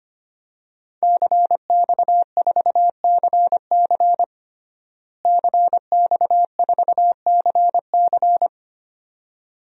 Morse code, heard as a keyed tone, about 25 wpm, 700 Hz. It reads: CX4CC CX4CC
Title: DXCC-100 Call Signs - 5 Characters - Repeated 2x - 25wpm